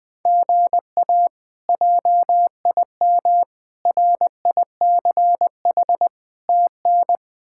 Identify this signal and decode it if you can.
Morse code, heard as a keyed tone, about 20 wpm, 700 Hz. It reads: GA JIM RICH TN
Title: K1USN SST QSO Practice - 20wpm